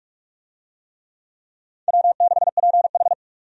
Morse code, heard as a keyed tone, about 47 wpm, 700 Hz. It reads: W6PH
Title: USA Call Signs - 4 Characters - 45wpm